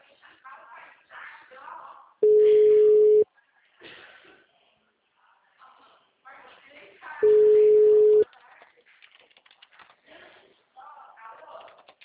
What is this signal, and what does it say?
Ringback tone (Europe, 425 Hz), with speech.
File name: out- -2031- - - .82780